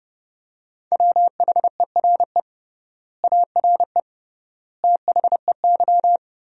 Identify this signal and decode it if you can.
Morse code, heard as a keyed tone, about 30 wpm, 700 Hz. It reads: WHERE ARE THEY
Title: 3x Word-Spacing - Sentences from Top 200 Words - 30wpm